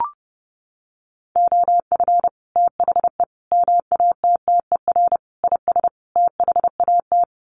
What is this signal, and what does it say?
Morse code, heard as a keyed tone, about 30 wpm, 700 Hz. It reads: OF THE MATTER IS THAT
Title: Sets of 5 Words Encore 30wpm